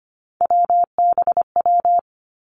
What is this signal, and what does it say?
Morse code, heard as a keyed tone, about 25 wpm, 700 Hz. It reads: WBW
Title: Three Challenging Letter-Number Combinations 25wpm